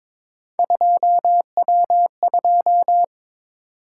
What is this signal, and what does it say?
Morse code, heard as a keyed tone, about 22 wpm, 700 Hz. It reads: 2W2
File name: practice / 22wpm / Three Letter-Number Combinations - 22wpm